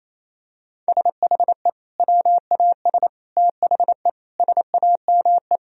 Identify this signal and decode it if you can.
Morse code, heard as a keyed tone, about 28 wpm, 700 Hz. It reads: SHE WAS THE SAME
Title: Sentences from Top 1000 Words - Review Part 3 - 28wpm